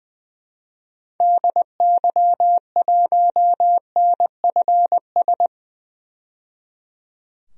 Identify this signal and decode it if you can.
Morse code, heard as a keyed tone, about 20 wpm, 700 Hz. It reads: DY1NFS
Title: DXCC-100 Call Signs - 6 Characters - 20wpm